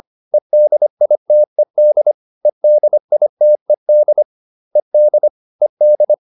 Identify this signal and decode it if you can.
Morse code, heard as a keyed tone, about 25 wpm, 600 Hz. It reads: EDITED EDITED ED ED